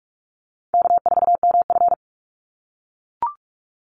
Morse code, heard as a keyed tone, about 44 wpm, 700 Hz. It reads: K4MF
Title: NA SOTA Chaser Callsigns 45wpm